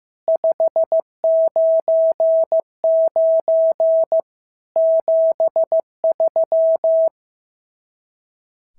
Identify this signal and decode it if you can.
Morse code, heard as a keyed tone, about 15 wpm, 650 Hz. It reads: 599 73
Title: SOTA QSO Practice 15wpm